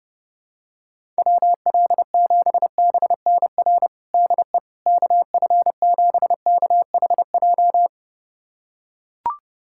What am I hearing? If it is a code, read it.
Morse code, heard as a keyed tone, about 31 wpm, 700 Hz. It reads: WL7BNR DE KF7KHJ
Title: USA Callsign DE Callsign - 6 Characters - 30wpm